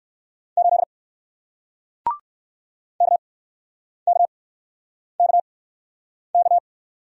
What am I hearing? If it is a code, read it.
Morse code, heard as a keyed tone, about 60 wpm, spaced as for 9 wpm, 700 Hz. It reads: 8 KKKK